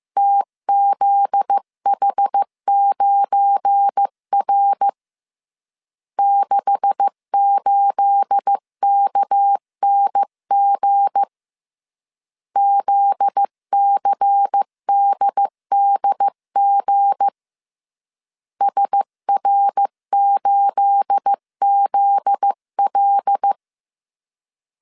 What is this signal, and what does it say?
Morse code, heard as a keyed tone, about 15 wpm, 800 Hz. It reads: TZH9R 68KNG ZCDDG SR8ZL
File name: CW / DET / 1 9 1500C S B 14wpm 3302